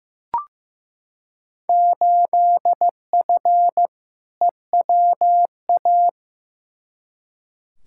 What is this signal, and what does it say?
Morse code, heard as a keyed tone, about 15 wpm, 700 Hz. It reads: 8F EWA
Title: Field Day Exchanges Encore - 15wpm